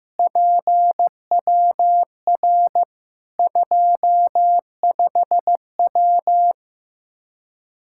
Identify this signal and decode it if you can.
Morse code, heard as a keyed tone, about 15 wpm, 700 Hz. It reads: PWR 25W
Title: Practice QSO - Exchange 2 - Info - 15wpm